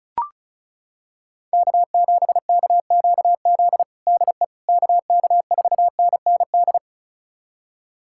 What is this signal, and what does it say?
Morse code, heard as a keyed tone, about 36 wpm, 700 Hz. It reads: K7KQZ DE KK4NND
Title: USA Callsign DE Callsign 35wpm